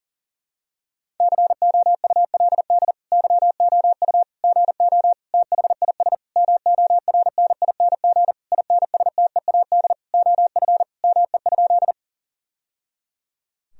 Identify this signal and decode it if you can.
Morse code, heard as a keyed tone, about 40 wpm, 700 Hz. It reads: COULD YOU GO THIS MORNING INSTEAD OF ME?
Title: Sentences from Top 1000 Words - Review Part 4 - 40wpm